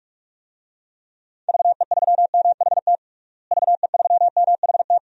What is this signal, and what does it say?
Morse code, heard as a keyed tone, about 42 wpm, 700 Hz. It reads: VE3MHT VE3MHT
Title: DXCC-100 Call Signs - 6 Characters - Repeated 3x - 45wpm